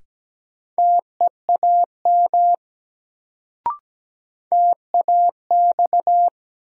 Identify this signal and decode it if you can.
Morse code, heard as a keyed tone, about 17 wpm, 700 Hz. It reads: TEAM TAX
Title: Sentences from Top 1500 Words - New Word Drill Alt - 17wpm